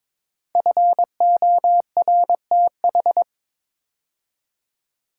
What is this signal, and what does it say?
Morse code, heard as a keyed tone, about 23 wpm, 700 Hz. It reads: FORTH
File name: practice / 22wpm / 5 Letter Words - First 3 Letters Given - 22wpm